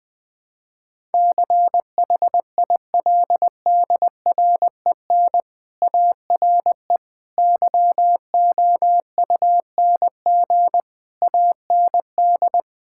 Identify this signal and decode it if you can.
Morse code, heard as a keyed tone, about 20 wpm, 700 Hz. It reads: CHILDREN ARE YOUNG AND
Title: Sentences from Top 1000 Words - Review Part 6 - 20wpm